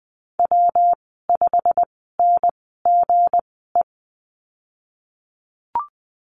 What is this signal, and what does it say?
Morse code, heard as a keyed tone, about 20 wpm, spaced as for 15 wpm, 700 Hz. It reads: W5NGE
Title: 179 Lesson 29g - Slash - Call Signs 20wpm